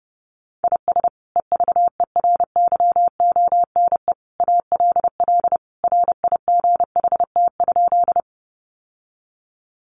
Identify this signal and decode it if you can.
Morse code, heard as a keyed tone, about 30 wpm, 700 Hz. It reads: IS EVERYONE ALL RIGHT?
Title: Sentences from Top 800 Words 30wpm